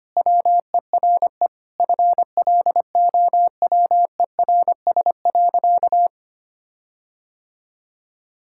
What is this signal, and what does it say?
Morse code, heard as a keyed tone, about 25 wpm, 700 Hz. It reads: WERE FLOWERS.
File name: practice / 25wpm / Sentences with 9-Word Phrases - 25wpm